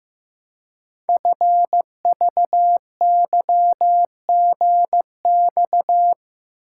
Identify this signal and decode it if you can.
Morse code, heard as a keyed tone, about 15 wpm, 700 Hz. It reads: FVYGX
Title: Code Groups - 5 Letters - 15wpm